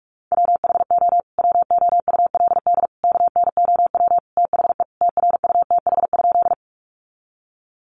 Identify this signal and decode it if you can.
Morse code, heard as a keyed tone, about 44 wpm, 700 Hz. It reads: WHO WOULD KNOW THE TRUTH?
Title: Sentences from Top 800 Words 45wpm